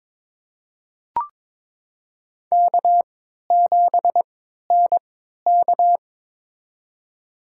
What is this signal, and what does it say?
Morse code, heard as a keyed tone, about 22 wpm, spaced as for 12 wpm, 700 Hz. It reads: K7NK
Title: USA Call Signs - 4 Characters - 3x Character Spacing - 22wpm